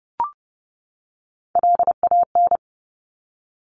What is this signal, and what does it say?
Morse code, heard as a keyed tone, about 30 wpm, 700 Hz. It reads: LAN
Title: Top Three Letter Combinations in Words 30wpm